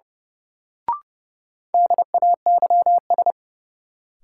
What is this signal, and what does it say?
Morse code, heard as a keyed tone, about 31 wpm, 700 Hz. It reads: DAYS
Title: Sets of 5 Words Encore - New Word Drill - 30wpm